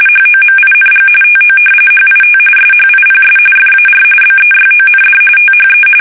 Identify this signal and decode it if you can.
RTTY, about 45.45 baud. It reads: HOMING TO HIT SHIPS FAR BEYOND THE
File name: fsk850 / battleshipnj